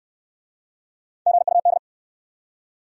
Morse code, heard as a keyed tone, about 61 wpm, 700 Hz. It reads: GUD